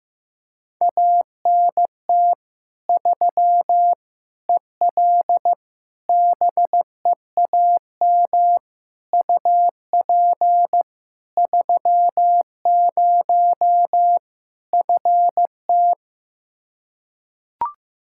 Morse code, heard as a keyed tone, about 16 wpm, 700 Hz. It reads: ANT 3 EL BEAM UP 30 FT
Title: Practice QSO - Exchange 2 - Info - 15wpm